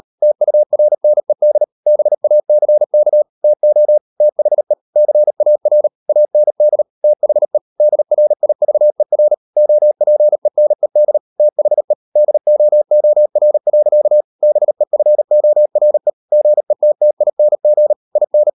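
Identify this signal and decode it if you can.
Morse code, heard as a keyed tone, about 38 wpm, 600 Hz. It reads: TURNED BACK TO THE CAR AND THE DRIVER OPENED THE DOOR. BEFORE GETTING IN
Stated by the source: Morse code practice at 38 WPM, sending Ken Follett's novel Never in English, from CW med Gnister.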